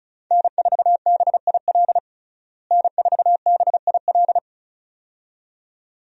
Morse code, heard as a keyed tone, about 36 wpm, 700 Hz. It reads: N4BIL N4BIL
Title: USA Call Signs - 5 Characters - Repeated 2x - 35wpm